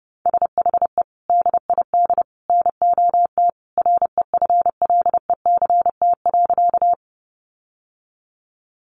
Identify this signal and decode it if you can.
Morse code, heard as a keyed tone, about 29 wpm, 700 Hz. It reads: SHE DID NOT REFLECT.
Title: Sentences from Top 1500 Words 30wpm